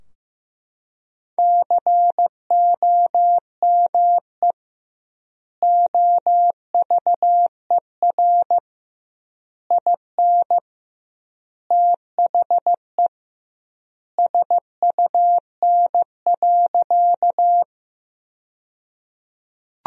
Morse code, heard as a keyed tone, about 16 wpm, 700 Hz. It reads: COME OVER IN THE SUN.
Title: 2x Word-Spacing - Sentences from Top 300 Words - 15wpm